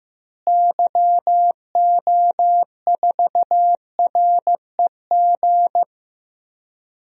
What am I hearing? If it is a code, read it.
Morse code, heard as a keyed tone, about 15 wpm, 700 Hz. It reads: YO4REG
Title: DXCC-100 Call Signs - 6 Characters - Repeated 3x - 15wpm